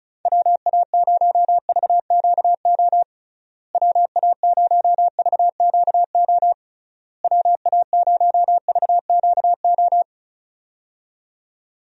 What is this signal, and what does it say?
Morse code, heard as a keyed tone, about 36 wpm, 700 Hz. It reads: WA0VQO WA0VQO WA0VQO
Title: USA Call Signs - 6 Characters - Repeated 3x - 35wpm